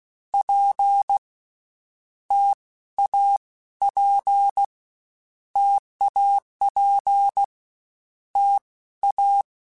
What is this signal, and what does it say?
Morse code, heard as a keyed tone, about 16 wpm, 800 Hz. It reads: P T A P TAP T A
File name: Difficult Combos / Difficult Combos-mp3s / T AOEI @15